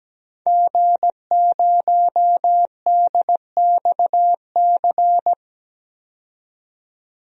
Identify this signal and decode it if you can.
Morse code, heard as a keyed tone, about 17 wpm, 700 Hz. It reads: G0DXC
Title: DXCC-100 Call Signs - 5 Characters - 17wpm